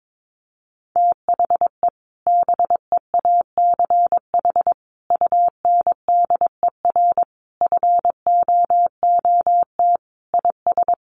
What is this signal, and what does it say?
Morse code, heard as a keyed tone, about 22 wpm, 700 Hz. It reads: THE BEACH UNDER FOOT IS
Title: Sentences from Top 1500 Words 22wpm